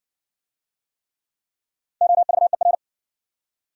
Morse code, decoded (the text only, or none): OVER